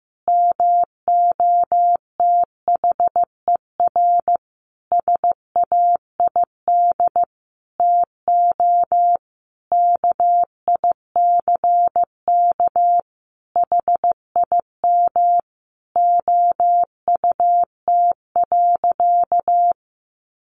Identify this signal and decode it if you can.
Morse code, heard as a keyed tone, about 15 wpm, 700 Hz. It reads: MOTHER SAID TO KICK HIM OUT.
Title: Sentences from Top 1300 Words 15wpm